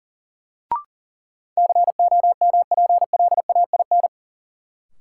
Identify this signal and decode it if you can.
Morse code, heard as a keyed tone, about 40 wpm, 700 Hz. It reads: COMPLAIN
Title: Sentences from Top 1300 Words - New Word Drill - 40wpm